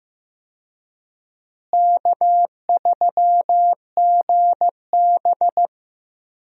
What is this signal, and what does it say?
Morse code, heard as a keyed tone, about 14 wpm, 700 Hz. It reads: K3GB
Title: USA Call Signs - 4 Characters - Repeated 3x - 15wpm